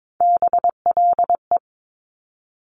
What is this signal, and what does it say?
Morse code, heard as a keyed tone, about 22 wpm, 700 Hz. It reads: BLE
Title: Top Three Letter Combinations in Words 22wpm